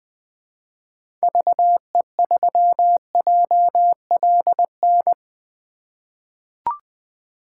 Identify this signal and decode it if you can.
Morse code, heard as a keyed tone, about 20 wpm, 700 Hz. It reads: VE3JLN
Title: 2023 POTA Activator Call Signs - 20wpm